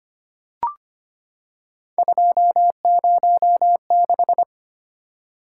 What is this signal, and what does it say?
Morse code, heard as a keyed tone, about 25 wpm, 700 Hz. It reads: 206
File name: Three Letter-Number Combinations - 25wpm